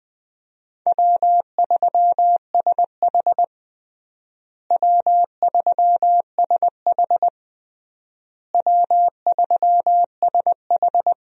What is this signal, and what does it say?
Morse code, heard as a keyed tone, about 21 wpm, 700 Hz. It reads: W3SH W3SH W3SH
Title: USA Call Signs - 4 Characters - Repeated 3x - 20wpm